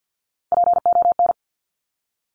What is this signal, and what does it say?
Morse code, heard as a keyed tone, about 50 wpm, 700 Hz. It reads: PON
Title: Top Three Letter Combinations in Words 50wpm